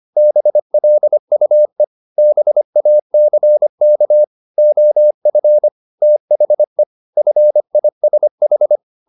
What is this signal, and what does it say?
Morse code, heard as a keyed tone, about 25 wpm, 600 Hz. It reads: BLUE BACK OF THE FISH
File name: the old man and the sea part 1 25wpm